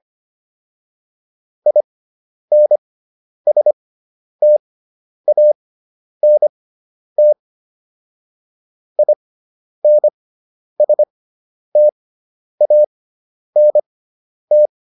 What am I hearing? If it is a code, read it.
Morse code, heard as a keyed tone, about 25 wpm, spaced as for 10 wpm, 600 Hz. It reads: INSTANT INSTANT